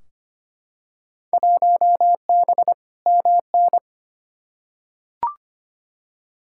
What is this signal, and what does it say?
Morse code, heard as a keyed tone, about 25 wpm, 700 Hz. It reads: 1B MN
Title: Field Day Exchanges Encore - 25wpm